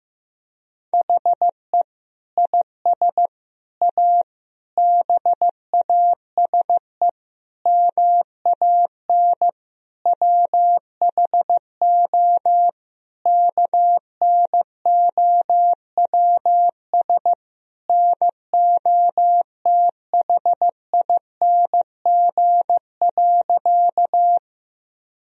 Morse code, heard as a keyed tone, about 16 wpm, 700 Hz. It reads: HE IS A BASE MAN WHO KNOWS NOTHING.
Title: Sentences from Top 1000 Words - Review Part 1 - 15wpm